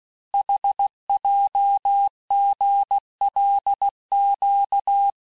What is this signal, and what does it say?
Morse code, heard as a keyed tone, about 16 wpm, 800 Hz. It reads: HJGLQ
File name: cours-du-lundi files / Cours-1586-texte-chiffre-vitesse-15mpm